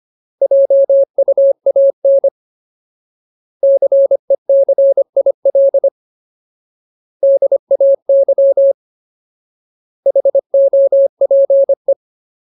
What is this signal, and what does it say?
Morse code, heard as a keyed tone, about 25 wpm, 550 Hz. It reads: JUAN CECIL DAY HOPE
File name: Ultimate Word List / 25WPM Ultimate